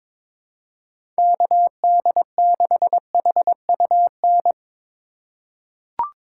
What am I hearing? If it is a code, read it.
Morse code, heard as a keyed tone, about 22 wpm, 700 Hz. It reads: KD6HUN
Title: USA Call Signs - 6 Characters - 2x Character Spacing - 22wpm